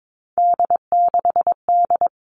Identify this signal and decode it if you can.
Morse code, heard as a keyed tone, about 22 wpm, 700 Hz. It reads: D6D
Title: All Cut Numbers - 3-Digit 22wpm